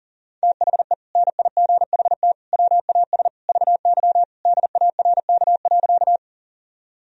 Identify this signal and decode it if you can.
Morse code, heard as a keyed tone, about 40 wpm, 700 Hz. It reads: THE NIGHT WAS VY DARK.
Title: Sentences from Top 500 Words and Abbreviations 40wpm